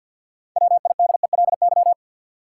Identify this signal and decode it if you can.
Morse code, heard as a keyed tone, about 50 wpm, 700 Hz. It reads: WIDELY
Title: Sentences from Top 1200 Words - New Word Drill Alt - 50wpm